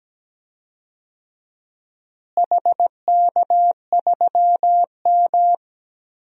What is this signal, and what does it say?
Morse code, heard as a keyed tone, about 17 wpm, 700 Hz. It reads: HK3M
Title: DXCC-100 Call Signs - 4 Characters - Repeated 2x - 17wpm